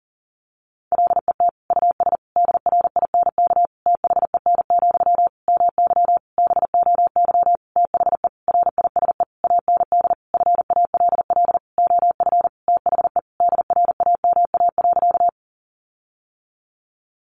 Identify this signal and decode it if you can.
Morse code, heard as a keyed tone, about 40 wpm, 700 Hz. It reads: LET US DRINK THEN, MY BOY THE RISE AND FALL OF THE DRAMA.
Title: Sentences from Top 1500 Words 40wpm